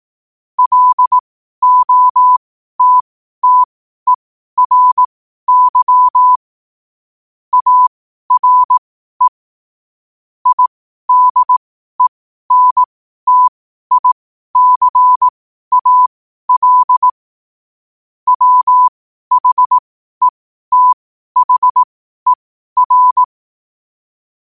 Morse code, heard as a keyed tone, about 18 wpm, spaced as for 12 wpm, 1000 Hz. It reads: LOTTERY ARE IDENTICAL WHETHER
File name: 12wpm